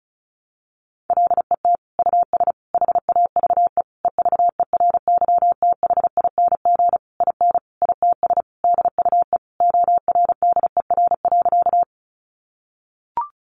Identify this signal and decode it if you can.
Morse code, heard as a keyed tone, about 34 wpm, 700 Hz. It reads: LET US HAVE EVERYTHING IN ITS DUE ORDER.
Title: Sentences from Top 900 Words 35wpm